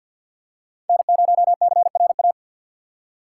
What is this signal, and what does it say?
Morse code, heard as a keyed tone, about 53 wpm, 700 Hz. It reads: N0CRA
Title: USA Call Signs - 5 Characters - 4x Character Spacing - 50wpm